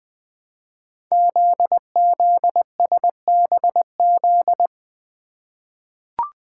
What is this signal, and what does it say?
Morse code, heard as a keyed tone, about 20 wpm, 700 Hz. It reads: ZZSBZ
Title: Code Groups - 5 Letters - 20wpm